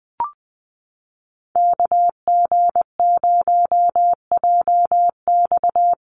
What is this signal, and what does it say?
Morse code, heard as a keyed tone, about 20 wpm, 700 Hz. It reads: KG0JX
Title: USA Call Signs Repeated 3x 20wpm